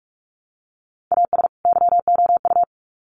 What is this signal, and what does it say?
Morse code, heard as a keyed tone, about 44 wpm, 700 Hz. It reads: AS YOU
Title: Sets of 2 Words 45wpm